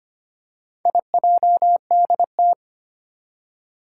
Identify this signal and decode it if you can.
Morse code, heard as a keyed tone, about 25 wpm, 700 Hz. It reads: IJDT
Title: Four Challenging Letter-Number Combinations 25wpm